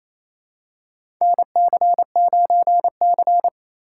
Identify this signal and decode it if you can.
Morse code, heard as a keyed tone, about 29 wpm, 700 Hz. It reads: NC9C
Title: USA Call Signs - 4 Characters - 8x Character Spacing - 28wpm